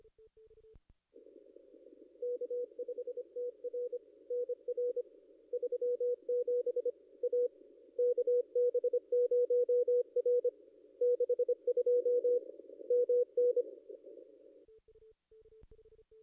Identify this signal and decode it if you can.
Morse code, heard as a keyed tone, about 25 wpm, 490 Hz. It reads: K5TR NR 37 A KB0R 62 MN
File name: contests / ss-cw-k5tr-op-n5rz -2200